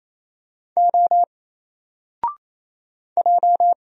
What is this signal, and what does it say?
Morse code, heard as a keyed tone, about 27 wpm, 700 Hz. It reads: O J
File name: Single-Letter-Number-28wpm